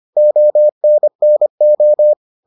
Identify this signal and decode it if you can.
Morse code, heard as a keyed tone, about 25 wpm, 600 Hz. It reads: ONNO